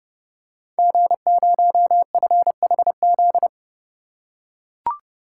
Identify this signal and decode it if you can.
Morse code, heard as a keyed tone, about 30 wpm, 700 Hz. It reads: G0FHZ
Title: Code Groups - 5 Letter:Numbers - 30wpm